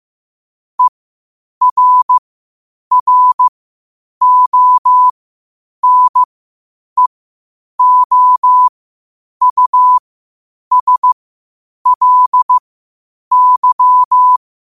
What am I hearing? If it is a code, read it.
Morse code, heard as a keyed tone, about 15 wpm, spaced as for 8 wpm, 1000 Hz. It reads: ERRONEOUSLY
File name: qotdfiles / 8wpm